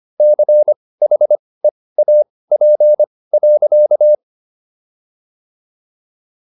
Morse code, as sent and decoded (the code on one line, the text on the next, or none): -.-. .... .- .--. .-.-.-
CHEAP.